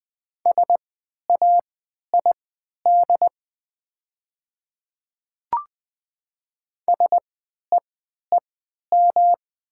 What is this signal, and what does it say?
Morse code, heard as a keyed tone, about 20 wpm, spaced as for 11 wpm, 700 Hz. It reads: SAID SEEM
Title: 3x Character Spacing - Top 300 Words - 20wpm